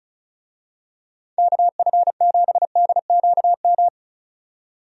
Morse code, heard as a keyed tone, about 36 wpm, 700 Hz. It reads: KF7DQM
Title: USA Call Signs - 6 Characters - 35wpm